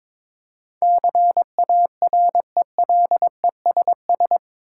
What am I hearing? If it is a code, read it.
Morse code, heard as a keyed tone, about 22 wpm, 700 Hz. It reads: CARELESS
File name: practice / 22wpm / Eight Letter Words - 22wpm